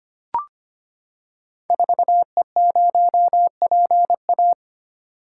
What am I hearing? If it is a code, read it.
Morse code, heard as a keyed tone, about 25 wpm, 700 Hz. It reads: 4E0PA
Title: Code Groups - 5 Letter:Numbers - 25wpm